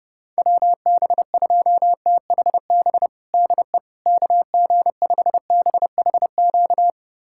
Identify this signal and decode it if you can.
Morse code, heard as a keyed tone, about 32 wpm, 700 Hz. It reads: WB2THB DE KG5BHQ
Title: USA Callsign DE Callsign 30wpm